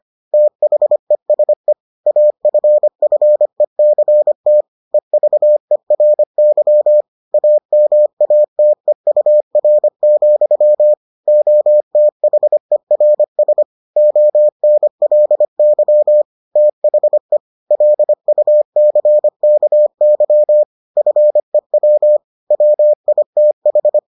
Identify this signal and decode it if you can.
Morse code, heard as a keyed tone, about 25 wpm, 600 Hz. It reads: THESE AFFECT EVERY AMATEUR, OTHERS ONLY THE LUCKY FEW WITH